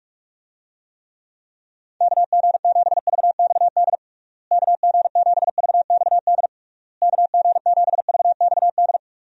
Morse code, heard as a keyed tone, about 46 wpm, 700 Hz. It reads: KG7VXD KG7VXD KG7VXD
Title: USA Call Signs - 6 Characters - Repeated 3x - 45wpm